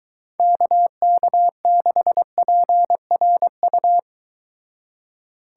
Morse code, heard as a keyed tone, about 23 wpm, 700 Hz. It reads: KK6PRU